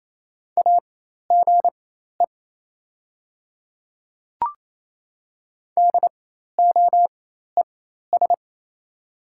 Morse code, heard as a keyed tone, about 28 wpm, spaced as for 13 wpm, 700 Hz. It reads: AGE DOES
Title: 4x Character Spacing - Top 500 Words - 28wpm